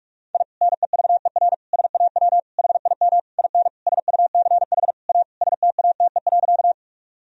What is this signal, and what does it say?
Morse code, heard as a keyed tone, about 42 wpm, 700 Hz. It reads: I NEVER SAW HIM IN SUCH A STATE.
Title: Sentences from Top 1000 Words - Review Part 2 - 45wpm